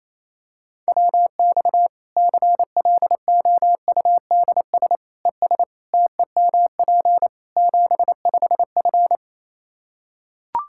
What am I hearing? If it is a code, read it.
Morse code, heard as a keyed tone, about 28 wpm, 700 Hz. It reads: WX CLOUDS ES TEMP 75F
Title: Practice QSO - Exchange 2 - Info - 28wpm